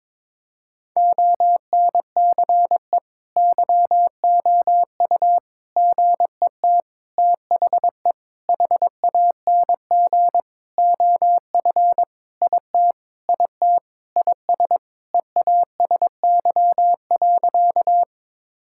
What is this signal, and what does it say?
Morse code, heard as a keyed tone, about 22 wpm, 700 Hz. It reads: ONCE YOU GET THE HANG OF IT IT IS EASY.
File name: Sentences with 7-Word Phrases - 22wpm